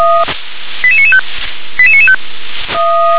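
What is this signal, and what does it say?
Five-tone selective call (ZVEI), heard as quick runs of five tones, tones of about 70 ms. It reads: 88995 88995
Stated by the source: ZVEI five-tone selective call received on SDR.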